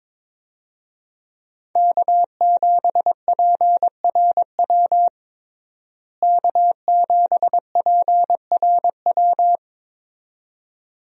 Morse code, heard as a keyed tone, about 21 wpm, 700 Hz. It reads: K7PRW K7PRW
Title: USA Call Signs - 5 Characters - Repeated 2x - 22wpm